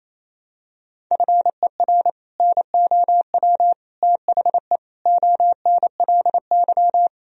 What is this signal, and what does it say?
Morse code, heard as a keyed tone, about 28 wpm, 700 Hz. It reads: FER NOW THE ONLY
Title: Sentences from Top 200 Words and Abbreviations 28wpm